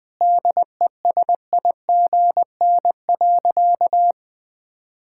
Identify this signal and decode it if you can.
Morse code, heard as a keyed tone, about 21 wpm, 700 Hz. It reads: DESIGN.
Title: Sentences from Top 1000 Words - Review Part 5 - 20wpm